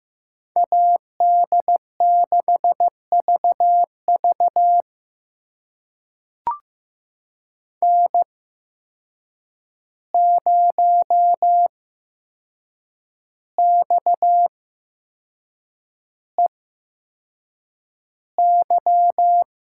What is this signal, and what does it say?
Morse code, heard as a keyed tone, about 15 wpm, spaced as for 6 wpm, 700 Hz. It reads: AD6VV N0XEY
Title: USA Call Signs - 5 Characters - 8x Character Spacing - 15wpm